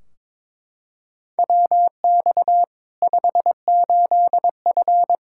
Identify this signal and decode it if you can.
Morse code, heard as a keyed tone, about 21 wpm, 700 Hz. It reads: WX 58F
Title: Practice QSO - Exchange 2 - Info - 22wpm